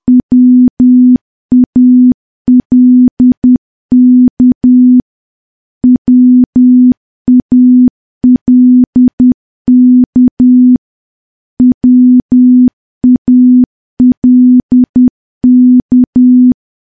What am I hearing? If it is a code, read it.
Morse code, heard as a keyed tone, about 10 wpm, 250 Hz. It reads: WALK WALK WALK
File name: morse / morse-training-10WPM